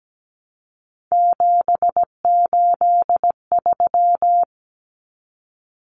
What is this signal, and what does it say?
Morse code, heard as a keyed tone, about 17 wpm, 700 Hz. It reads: 783
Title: Common Cut Numbers - 3-Digit 17wpm